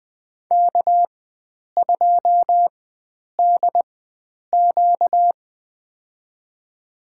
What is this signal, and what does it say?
Morse code, heard as a keyed tone, about 20 wpm, spaced as for 9 wpm, 700 Hz. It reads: K2DQ
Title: USA Call Signs - 4x Character Spacing - 20wpm